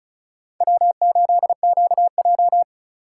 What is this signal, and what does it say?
Morse code, heard as a keyed tone, about 46 wpm, 700 Hz. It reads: W8QJ